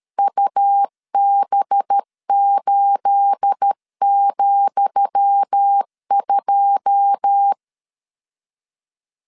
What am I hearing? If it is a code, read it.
Morse code, heard as a keyed tone, about 13 wpm, 800 Hz. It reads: UB8,2
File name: LEZIONI / 10 SIGNS / 10D COMA 12wpm 2544